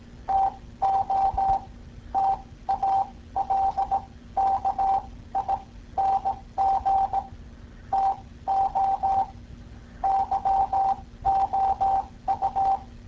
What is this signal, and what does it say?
Morse code, heard as a keyed tone, about 18 wpm, spaced as for 14 wpm, 780 Hz. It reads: TO TALKING TO YOU